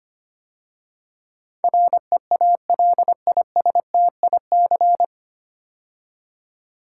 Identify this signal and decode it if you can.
Morse code, heard as a keyed tone, about 25 wpm, 690 Hz. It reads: REALISTIC
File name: practice / 25wpm / 9 Letter Words - First 3 Letters Given - 25wpm